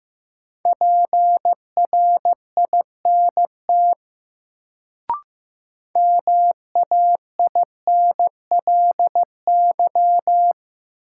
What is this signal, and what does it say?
Morse code, heard as a keyed tone, about 15 wpm, 700 Hz. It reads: PRINT MAINLY
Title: Sentences from Top 1400 Words - New Word Drill Alt - 15wpm